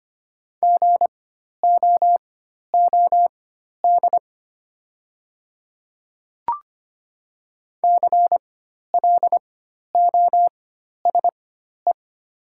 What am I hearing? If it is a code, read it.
Morse code, heard as a keyed tone, about 25 wpm, spaced as for 12 wpm, 700 Hz. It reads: GOOD CLOSE
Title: 4x Character Spacing - Top 500 Words - 25wpm